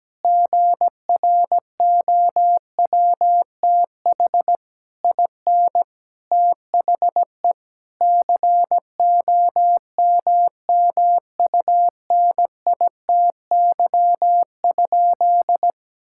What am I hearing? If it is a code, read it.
Morse code, heard as a keyed tone, about 17 wpm, 700 Hz. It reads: GROWTH IN THE COMMUNITY?
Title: Sentences from Top 1000 Words - Review Part 5 - 17wpm